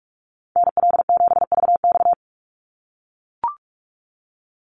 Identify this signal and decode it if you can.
Morse code, heard as a keyed tone, about 44 wpm, 700 Hz. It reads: NL7VX
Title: Call Signs - 45wpm